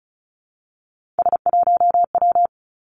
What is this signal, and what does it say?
Morse code, heard as a keyed tone, about 36 wpm, 700 Hz. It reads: S1W
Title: Three Challenging Letter-Number Combinations 35wpm